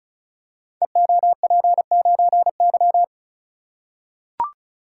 Morse code, heard as a keyed tone, about 36 wpm, 700 Hz. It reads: EOP9Y
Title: Code Groups - 5 Letter:Numbers - 35wpm